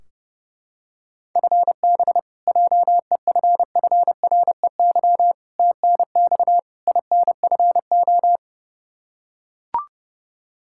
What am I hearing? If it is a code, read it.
Morse code, heard as a keyed tone, about 32 wpm, 700 Hz. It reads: FB JEFFREY TNX INFO
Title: K1USN SST QSO Practice 30wpm